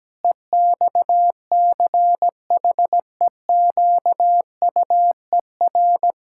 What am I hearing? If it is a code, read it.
Morse code, heard as a keyed tone, about 17 wpm, 700 Hz. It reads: EXCHEQUER
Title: Pangram Challenge v2 - 17wpm